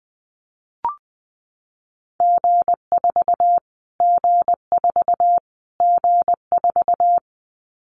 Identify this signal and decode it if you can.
Morse code, heard as a keyed tone, about 20 wpm, 700 Hz. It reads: G4 G4 G4
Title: Callsign Prefixes for Top 25 DXCC - Repeated 3x - 20wpm